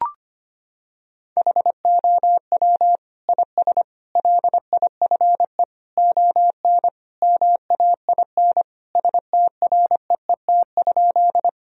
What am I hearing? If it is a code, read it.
Morse code, heard as a keyed tone, about 25 wpm, 700 Hz. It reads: HOW IS LIFE ON MAIN STREET?
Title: Sentences from Top 1000 Words - Review Part 2 - 25wpm